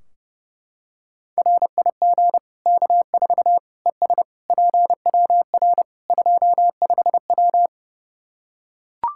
Morse code, heard as a keyed tone, about 30 wpm, 700 Hz. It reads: RIG K4 ES PWR 25W
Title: Practice QSO - Exchange 2 - Info - 30wpm